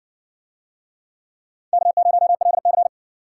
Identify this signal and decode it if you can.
Morse code, heard as a keyed tone, about 58 wpm, 700 Hz. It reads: K0LC